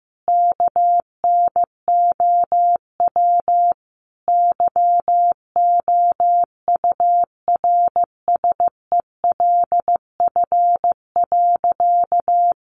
Morse code, heard as a keyed tone, about 15 wpm, 700 Hz. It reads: KNOW YOURSELF.
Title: Sentences from Top 600 Words 15wpm